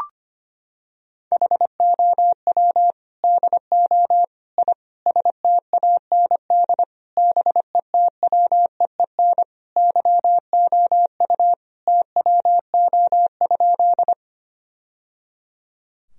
Morse code, heard as a keyed tone, about 25 wpm, 700 Hz. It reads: HOW DO I STAND BETWEEN YOU TWO?
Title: Sentences from Top 1000 Words - Review Part 1 - 25wpm